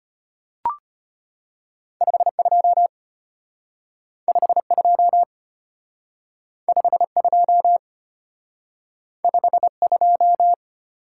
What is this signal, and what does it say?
Morse code, heard as a keyed tone, about 34 wpm, 700 Hz. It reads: 52 52 52 52